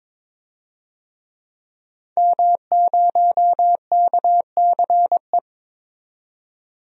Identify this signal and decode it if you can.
Morse code, heard as a keyed tone, about 22 wpm, 700 Hz. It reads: M0KCE
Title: DXCC-100 Call Signs - 5 Characters - 22wpm